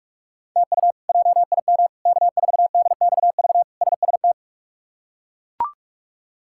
Morse code, heard as a keyed tone, about 41 wpm, 700 Hz. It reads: TU JIM K4DXV SST
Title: K1USN SST QSO Practice - 45wpm